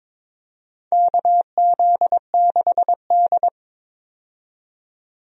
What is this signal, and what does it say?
Morse code, heard as a keyed tone, about 22 wpm, 700 Hz. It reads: KZ6D
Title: USA Call Signs - 4 Characters - 22wpm